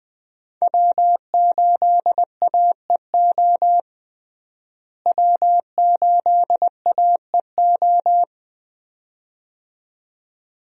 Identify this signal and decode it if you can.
Morse code, heard as a keyed tone, about 20 wpm, 700 Hz. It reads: W8AEO W8AEO
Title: USA Call Signs - 5 Characters - Repeated 2x - 20wpm